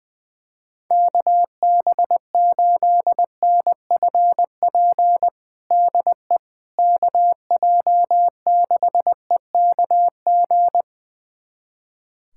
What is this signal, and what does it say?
Morse code, heard as a keyed tone, about 21 wpm, 700 Hz. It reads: KB8NFP DE KJ6EKG
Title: USA Callsign DE Callsign - 6 Characters - 20wpm